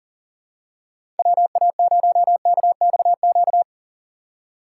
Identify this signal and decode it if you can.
Morse code, heard as a keyed tone, about 40 wpm, 700 Hz. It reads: WA0KXQ
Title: USA Call Signs - 6 Characters - 2x Character Spacing - 40wpm